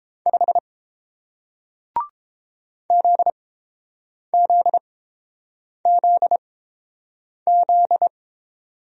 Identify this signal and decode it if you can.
Morse code, heard as a keyed tone, about 29 wpm, spaced as for 7 wpm, 700 Hz. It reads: 5 ZZZZ